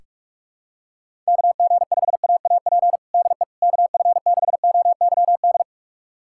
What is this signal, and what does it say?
Morse code, heard as a keyed tone, about 46 wpm, 700 Hz. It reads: KG5AAP DE KF6OYD
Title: USA Callsign DE Callsign - 6 Characters - 45wpm